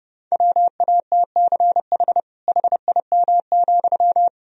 Morse code, heard as a keyed tone, about 30 wpm, 700 Hz. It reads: WATCH HIM,
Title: Sentences from Top 300 Words and Abbreviations 30wpm